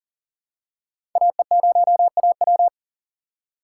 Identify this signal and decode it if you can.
Morse code, heard as a keyed tone, about 40 wpm, 700 Hz. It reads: AE0AW
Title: USA Call Signs - 5 Characters - 2x Character Spacing - 40wpm